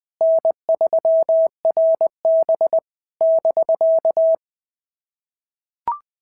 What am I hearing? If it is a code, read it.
Morse code, heard as a keyed tone, about 20 wpm, 660 Hz. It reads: N3RB <BK>
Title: SOTA QSO Practice 20wpm